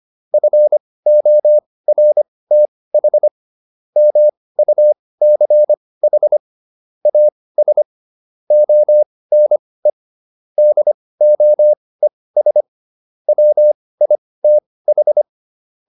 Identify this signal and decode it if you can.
Morse code, heard as a keyed tone, about 25 wpm, spaced as for 18 wpm, 600 Hz. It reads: FORTH MUCH AS ONE DOES WITH